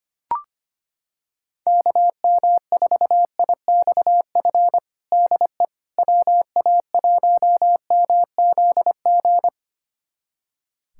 Morse code, heard as a keyed tone, about 25 wpm, 700 Hz. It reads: KM4IXF DE WA1MZG
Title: USA Callsign DE Callsign 25wpm